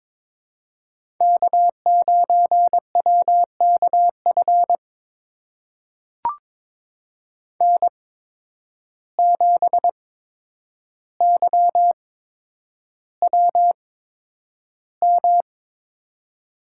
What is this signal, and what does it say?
Morse code, heard as a keyed tone, about 22 wpm, spaced as for 9 wpm, 700 Hz. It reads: K9WKF N7YWM
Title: USA Call Signs - 5 Characters - 8x Character Spacing - 22wpm